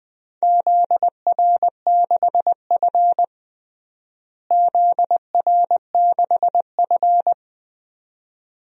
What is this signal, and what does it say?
Morse code, heard as a keyed tone, about 20 wpm, 700 Hz. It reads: ZR6F ZR6F
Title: DXCC-100 Call Signs - 4 Characters - Repeated 2x - 20wpm